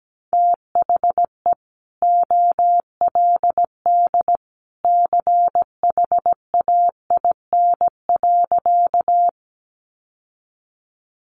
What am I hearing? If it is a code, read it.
Morse code, heard as a keyed tone, about 17 wpm, 700 Hz. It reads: THE OLD CHAIN.
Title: Sentences from Top 1500 Words 17wpm